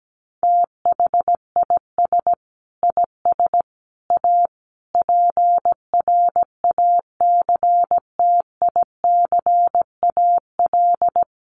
Morse code, heard as a keyed tone, about 17 wpm, 700 Hz. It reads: THIS IS A PRACTICAL
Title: Sentences from Top 1400 Words 17wpm